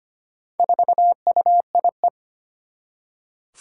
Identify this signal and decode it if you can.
Morse code, heard as a keyed tone, about 25 wpm, 700 Hz. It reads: 4UIE
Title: Four Challenging Letter-Number Combinations 25wpm